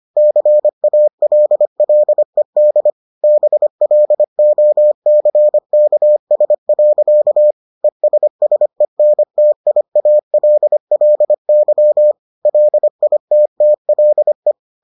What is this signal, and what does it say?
Morse code, heard as a keyed tone, about 25 wpm, 600 Hz. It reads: CALLED BLOCKS. ESSENTIALLY LITTLE